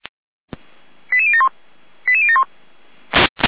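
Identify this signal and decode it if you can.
Five-tone selective call (ZVEI), heard as quick runs of five tones, tones of about 70 ms. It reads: 88071 88071
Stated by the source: Five-tone ZVEI selective call received by SDR.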